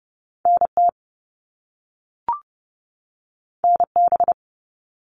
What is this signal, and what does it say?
Morse code, heard as a keyed tone, about 30 wpm, 700 Hz. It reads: NT NB
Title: Canadian Provinces and Territories 30wpm